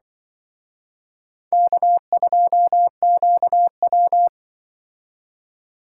Morse code, heard as a keyed tone, about 26 wpm, 700 Hz. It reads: K2QW